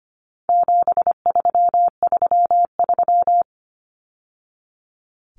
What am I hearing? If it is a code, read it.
Morse code, heard as a keyed tone, about 25 wpm, 700 Hz. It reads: 7333
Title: Numbers - 4-Digit 25wpm